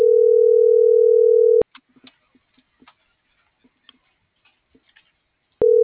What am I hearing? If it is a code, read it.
Ringback tone (North America), 440 + 480 Hz.